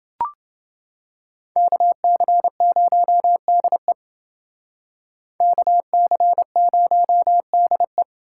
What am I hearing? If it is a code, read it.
Morse code, heard as a keyed tone, about 28 wpm, 700 Hz. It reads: KC0DE KC0DE